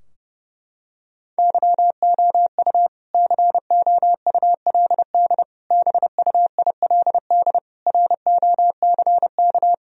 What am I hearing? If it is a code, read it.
Morse code, heard as a keyed tone, about 32 wpm, 700 Hz. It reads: YOU COULD BUILD ROCK
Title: Sentences from Top 1000 Words - Review Part 2 - 30wpm